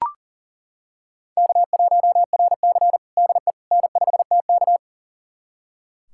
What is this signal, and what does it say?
Morse code, heard as a keyed tone, about 40 wpm, 700 Hz. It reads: K1RC DE N5TK
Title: USA Callsign DE Callsign - 4 Characters - 40wpm